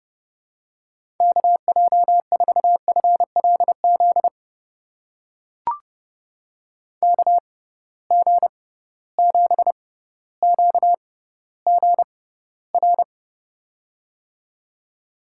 Morse code, heard as a keyed tone, about 32 wpm, spaced as for 16 wpm, 700 Hz. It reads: KJ4FLZ KG7QGR
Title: USA Call Signs - 6 Characters - 6x Character Spacing - 30wpm